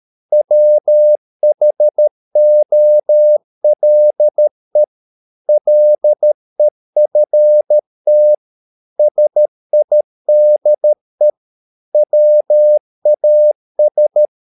Morse code, heard as a keyed tone, about 13 wpm, 600 Hz. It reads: WHOLE LEFT SIDE WAS
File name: mp3 files / Lexiphiles - 13wpm